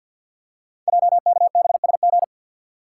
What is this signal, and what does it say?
Morse code, heard as a keyed tone, about 50 wpm, 690 Hz. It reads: JKBSG
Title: Code Groups - 5 Letters - 50wpm